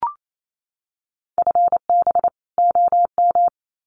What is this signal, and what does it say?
Morse code, heard as a keyed tone, about 28 wpm, 700 Hz. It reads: FB OM
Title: Common QSO Phrases 28wpm